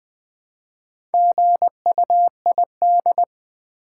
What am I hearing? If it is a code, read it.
Morse code, heard as a keyed tone, about 20 wpm, 700 Hz. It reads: GUID
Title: Build A Word - 1 Letter At A Time - 5-Letter Words - 20wpm